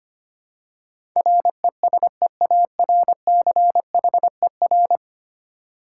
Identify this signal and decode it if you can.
Morse code, heard as a keyed tone, about 25 wpm, 700 Hz. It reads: RESEARCHER
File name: 10 Letter Words - First 3 Letters Given - 25wpm